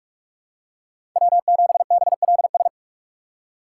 Morse code, heard as a keyed tone, about 46 wpm, 700 Hz. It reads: W 7 B L S
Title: USA Call Signs - 5 Characters - 6x Character Spacing - 45wpm